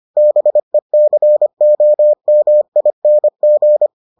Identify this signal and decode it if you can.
Morse code, heard as a keyed tone, about 25 wpm, 600 Hz. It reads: BECOMING